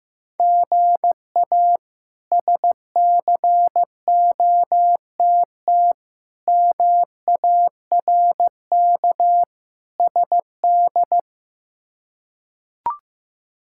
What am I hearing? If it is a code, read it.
Morse code, heard as a keyed tone, about 16 wpm, 700 Hz. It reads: GA SCOTT MARK SD
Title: K1USN SST QSO Practice - 15wpm